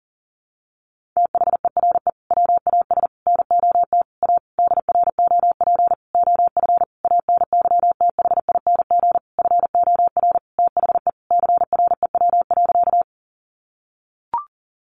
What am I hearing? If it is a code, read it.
Morse code, heard as a keyed tone, about 40 wpm, 700 Hz. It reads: THERE WAS NOT A DROP OF ANYTHING FOR THE CREW.
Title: Sentences from Top 1500 Words 40wpm